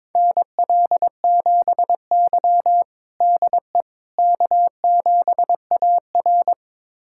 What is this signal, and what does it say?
Morse code, heard as a keyed tone, about 22 wpm, 700 Hz. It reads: NL7Y DE K7AR
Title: USA Callsign DE Callsign - 4 Characters - 22wpm